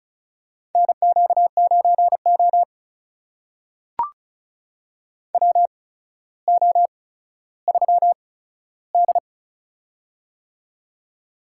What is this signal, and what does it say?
Morse code, heard as a keyed tone, about 36 wpm, spaced as for 15 wpm, 700 Hz. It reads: NQ9O WO3D
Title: USA Call Signs - 4 Characters - 8x Character Spacing - 35wpm